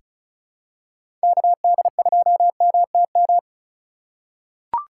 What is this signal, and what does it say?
Morse code, heard as a keyed tone, about 34 wpm, 700 Hz. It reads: KD2MTM
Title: USA Call Signs - 6 Characters - 3x Character Spacing - 35wpm